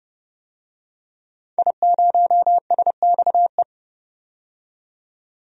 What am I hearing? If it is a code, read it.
Morse code, heard as a keyed tone, about 29 wpm, 700 Hz. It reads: I0SXE
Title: DXCC-100 Call Signs - 5 Characters - 30wpm